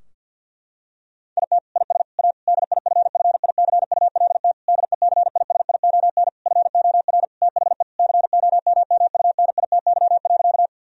Morse code, heard as a keyed tone, about 52 wpm, 700 Hz. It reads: IT IS A DIFFICULT DECISION FOR THE COMMUNITY.
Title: Sentences from Top 1000 Words - Review Part 4 - 50wpm